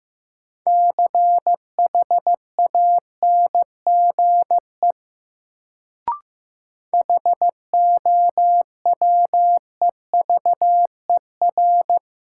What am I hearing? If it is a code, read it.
Morse code, heard as a keyed tone, about 15 wpm, 700 Hz. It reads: CHANGE HOWEVER
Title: Sentences from Top 600 Words - New Word Drill - 15wpm